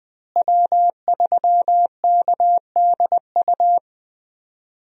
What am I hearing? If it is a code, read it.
Morse code, heard as a keyed tone, about 20 wpm, 700 Hz. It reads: W3KDU
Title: USA Call Signs - 5 Characters - Repeated 3x - 20wpm